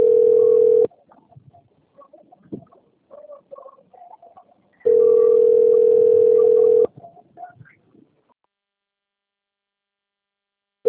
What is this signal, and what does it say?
Ringback tone (North America), 440 + 480 Hz.